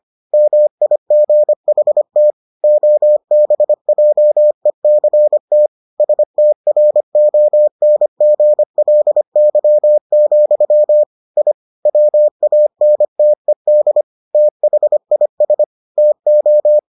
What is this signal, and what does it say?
Morse code, heard as a keyed tone, about 25 wpm, 600 Hz. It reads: MIGHT OBJECT STRONGLY, I WANTED THIS TO